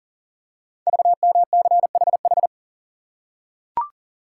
Morse code, heard as a keyed tone, about 39 wpm, 690 Hz. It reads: VMCHH
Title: Code Groups - 5 Letter:Numbers - 40wpm